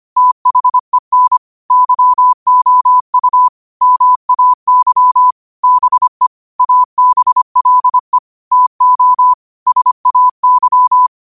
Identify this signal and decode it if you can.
Morse code, heard as a keyed tone, about 25 wpm, 1000 Hz. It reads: THEN YOU MAY BE ABLE TO SAY
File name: qotdfiles / 25wpm